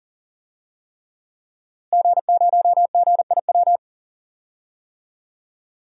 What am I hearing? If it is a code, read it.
Morse code, heard as a keyed tone, about 40 wpm, 700 Hz. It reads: G0GIW
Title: DXCC-100 Call Signs - 5 Characters - Repeated 3x - 40wpm